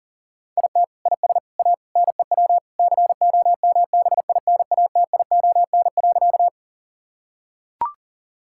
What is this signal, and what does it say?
Morse code, heard as a keyed tone, about 40 wpm, 700 Hz. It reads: IT IS A NEW COMBINATION.
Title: Sentences from Top 1000 Words - Review Part 6 - 40wpm